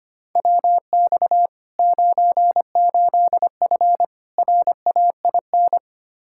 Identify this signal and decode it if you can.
Morse code, heard as a keyed tone, about 25 wpm, 700 Hz. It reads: WX 98F RAIN
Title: Practice QSO - Exchange 2 - Info - 25wpm